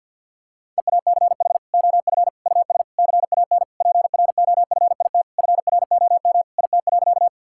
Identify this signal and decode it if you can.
Morse code, heard as a keyed tone, about 50 wpm, 700 Hz. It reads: EACH OF US CAN PROFIT FROM IT.
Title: Sentences from Top 1000 Words - Review Part 5 - 50wpm